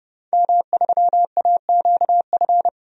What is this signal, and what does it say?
Morse code, heard as a keyed tone, about 30 wpm, 700 Hz. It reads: M3AQF
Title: DXCC-100 Call Signs - 5 Characters - 30wpm